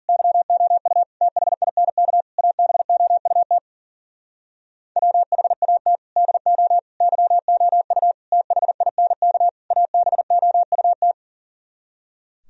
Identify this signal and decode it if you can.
Morse code, heard as a keyed tone, about 54 wpm, 700 Hz. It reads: YOU THINK ABOUT WHAT DO YOU THINK ABOUT